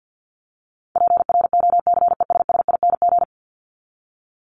Morse code, heard as a keyed tone, about 48 wpm, 700 Hz. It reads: PROCESSING
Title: ING Suffix - Top 100 Words with Base Words 50wpm